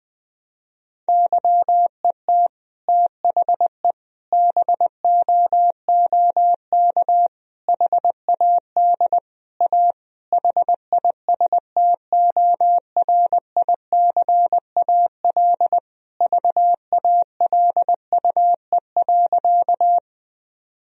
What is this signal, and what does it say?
Morse code, heard as a keyed tone, about 20 wpm, 700 Hz. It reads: YET THE BOOK HAD A HISTORICAL VALUE.
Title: Sentences from Top 1000 Words - Review Part 4 - 20wpm